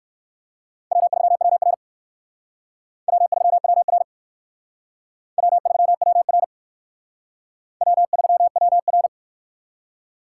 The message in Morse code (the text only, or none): W3WR W3WR W3WR W3WR